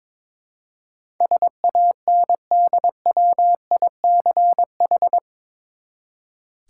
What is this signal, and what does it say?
Morse code, heard as a keyed tone, about 22 wpm, 700 Hz. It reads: SANDWICH
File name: practice / 22wpm / Eight Letter Words - 22wpm